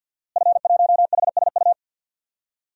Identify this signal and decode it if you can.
Morse code, heard as a keyed tone, about 49 wpm, 700 Hz. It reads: F1HSU
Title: DXCC-100 Call Signs - 5 Characters - Repeated 2x - 50wpm